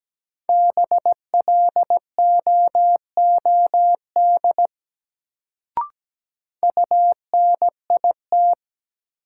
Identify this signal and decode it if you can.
Morse code, heard as a keyed tone, about 17 wpm, 700 Hz. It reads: BLOOD UNIT
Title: Sentences from Top 1000 Words - Review Part 2 - New Word Drill - 17wpm